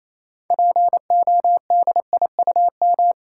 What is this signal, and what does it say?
Morse code, heard as a keyed tone, about 27 wpm, 690 Hz. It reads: PODIUM
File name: Easy Words - Speed-Racing with Hints - 22wpm